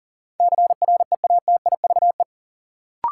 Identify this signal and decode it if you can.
Morse code, heard as a keyed tone, about 40 wpm, 700 Hz. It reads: CREATIVE
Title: Sentences from Top 1000 Words - Review Part 5 - New Word Drill - 40wpm